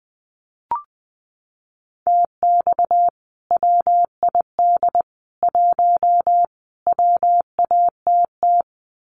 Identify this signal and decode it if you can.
Morse code, heard as a keyed tone, about 20 wpm, 700 Hz. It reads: TX WID 1 WATT
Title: Common QSO Phrases 20wpm